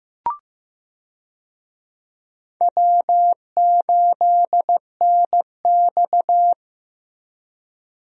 Morse code, heard as a keyed tone, about 15 wpm, 700 Hz. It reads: W8NX
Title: USA Call Signs - 4 Characters - Repeated 3x - 15wpm